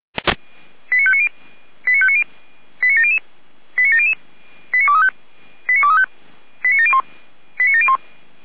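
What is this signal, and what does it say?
Five-tone selective call (ZVEI), heard as quick runs of five tones, tones of about 70 ms. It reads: 89590 89590 89700 89700 89235 89235 89701 89701
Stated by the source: ZVEI five-tone selective call received on an SDR.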